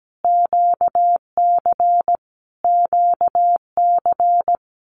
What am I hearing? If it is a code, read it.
Morse code, heard as a keyed tone, about 17 wpm, 700 Hz. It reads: QC QC
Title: Canadian Provinces and Territories - Repeated 3x - 17wpm